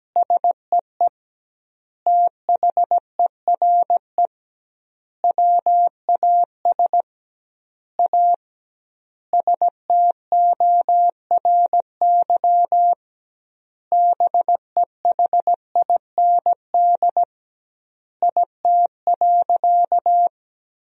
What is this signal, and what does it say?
Morse code, heard as a keyed tone, about 17 wpm, 700 Hz. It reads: SEE THERE WAS A STORY BEHIND IT.
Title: 2x Word-Spacing - Sentences from Top 500 Words - 17wpm